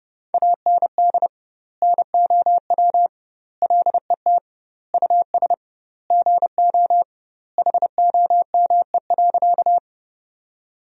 Morse code, heard as a keyed tone, about 30 wpm, 700 Hz. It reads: AND NOW LET US GO HOME.
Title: 2x Word-Spacing - Sentences from Top 300 Words - 30wpm